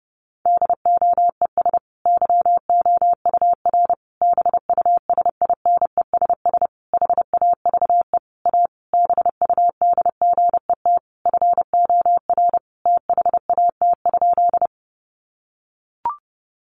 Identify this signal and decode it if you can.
Morse code, heard as a keyed tone, about 30 wpm, 700 Hz. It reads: DOES YOUR BUSINESS HAVE A BUDGET FOR THAT?
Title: Sentences from Top 1200 Words 30wpm